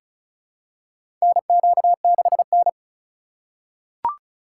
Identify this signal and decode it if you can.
Morse code, heard as a keyed tone, about 36 wpm, 700 Hz. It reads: NQ6N
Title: USA Call Signs - 4 Characters - 4x Character Spacing - 35wpm